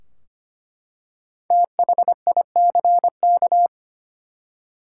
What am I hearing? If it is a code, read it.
Morse code, heard as a keyed tone, about 25 wpm, 700 Hz. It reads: THICK
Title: Five Letter Words - 25wpm